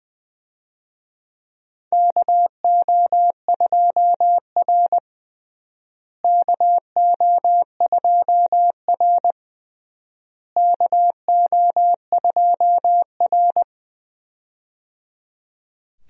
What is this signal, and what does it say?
Morse code, heard as a keyed tone, about 19 wpm, 700 Hz. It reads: KO2R KO2R KO2R
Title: USA Call Signs - 4 Characters - Repeated 3x - 20wpm